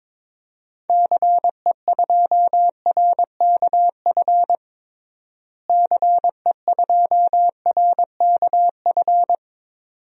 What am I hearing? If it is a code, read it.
Morse code, heard as a keyed tone, about 22 wpm, 700 Hz. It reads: CE2RKF CE2RKF
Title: DXCC-100 Call Signs - 6 Characters - Repeated 2x - 22wpm